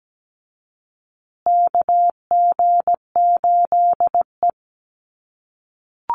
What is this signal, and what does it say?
Morse code, heard as a keyed tone, about 17 wpm, 700 Hz. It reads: KG8E
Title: USA Call Signs Repeated 3x 17wpm